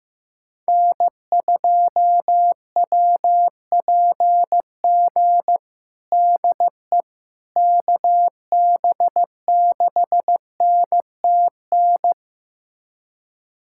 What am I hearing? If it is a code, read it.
Morse code, heard as a keyed tone, about 15 wpm, 700 Hz. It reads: N2WPG DE KB6NTN
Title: USA Callsign DE Callsign 15wpm